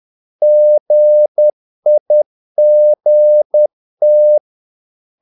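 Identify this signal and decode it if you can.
Morse code, heard as a keyed tone, about 10 wpm, 600 Hz. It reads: GIGT
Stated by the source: Morse practice of short Danish words at 10 wpm, from CW med Gnister.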